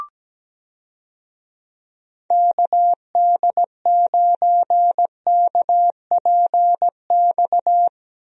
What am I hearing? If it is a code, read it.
Morse code, heard as a keyed tone, about 17 wpm, 700 Hz. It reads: KD9KPX
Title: USA Call Signs - 6 Characters - Repeated 3x - 17wpm